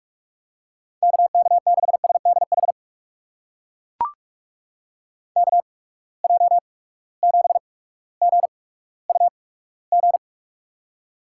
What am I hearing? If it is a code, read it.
Morse code, heard as a keyed tone, about 43 wpm, spaced as for 19 wpm, 700 Hz. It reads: KK6SDH KJ7GUG
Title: USA Call Signs - 6 Characters - 8x Character Spacing - 45wpm